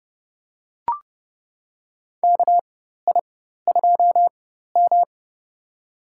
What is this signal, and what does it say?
Morse code, heard as a keyed tone, about 30 wpm, spaced as for 14 wpm, 700 Hz. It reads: KI2M
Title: USA Call Signs - 4x Character Spacing - 30wpm